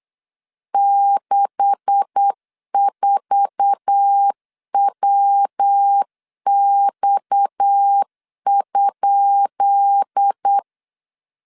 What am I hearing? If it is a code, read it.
Morse code, heard as a keyed tone, about 8 wpm, 800 Hz. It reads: 64WX?
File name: LEZIONI / 10 SIGNS / 10B INT 8wpm 3534